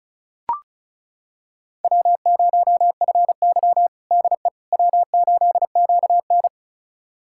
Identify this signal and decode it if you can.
Morse code, heard as a keyed tone, about 34 wpm, 700 Hz. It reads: W0FY DE W8QN
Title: USA Callsign DE Callsign - 4 Characters - 35wpm